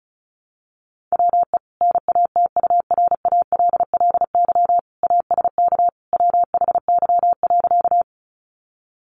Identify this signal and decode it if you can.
Morse code, heard as a keyed tone, about 34 wpm, 700 Hz. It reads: WE NATURALLY ASK WHY.
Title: Sentences from Top 1300 Words 35wpm